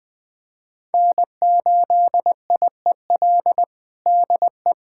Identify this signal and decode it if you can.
Morse code, heard as a keyed tone, about 21 wpm, 700 Hz. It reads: N8IEL DE
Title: USA Callsign DE Callsign 20wpm